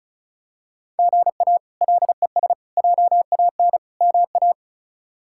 Morse code, heard as a keyed tone, about 34 wpm, 700 Hz. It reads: GA LES JAN MA
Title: K1USN SST QSO Practice - 35wpm